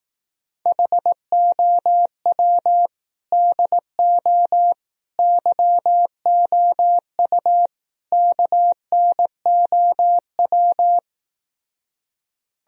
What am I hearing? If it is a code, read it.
Morse code, heard as a keyed tone, about 18 wpm, 700 Hz. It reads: HOW DO YOU KNOW
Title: Speed-Racing - Sets of 4 Words 15wpm